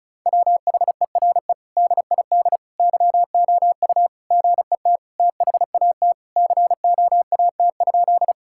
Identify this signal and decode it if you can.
Morse code, heard as a keyed tone, about 34 wpm, 700 Hz. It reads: WHERE DID YOU GET THAT COAT?
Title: Sentences from Top 1600 Words 35wpm